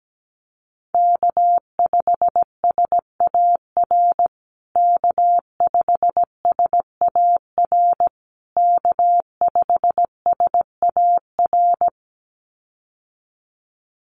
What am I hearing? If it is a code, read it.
Morse code, heard as a keyed tone, about 17 wpm, 700 Hz. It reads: K5SAR K5SAR K5SAR
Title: USA Call Signs Repeated 3x 17wpm